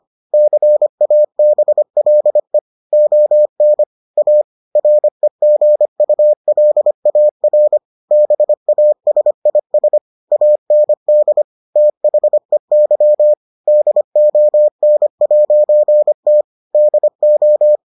Morse code, heard as a keyed tone, about 25 wpm, 600 Hz. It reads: CABLE ON A REGULAR BASIS AND THEY DON'T DO